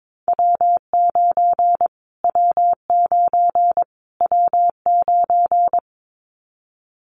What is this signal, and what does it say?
Morse code, heard as a keyed tone, about 22 wpm, 700 Hz. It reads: W9 W9 W9
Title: Callsign Prefixes for Top 25 DXCC - Repeated 3x - 22wpm